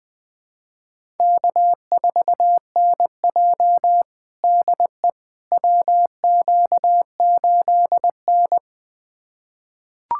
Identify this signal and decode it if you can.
Morse code, heard as a keyed tone, about 20 wpm, 700 Hz. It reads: K4NJ DE WQ8N
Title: USA Callsign DE Callsign - 4 Characters - 20wpm